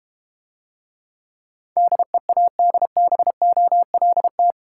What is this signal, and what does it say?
Morse code, heard as a keyed tone, about 32 wpm, 690 Hz. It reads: DEADBOLT